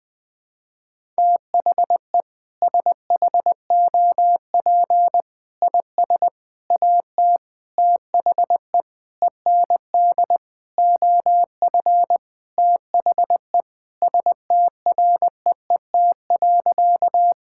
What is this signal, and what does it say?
Morse code, heard as a keyed tone, about 20 wpm, 700 Hz. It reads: THE SHOP IS AT THE END OF THE STREET.
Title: Sentences with 5-Word Phrases - 20wpm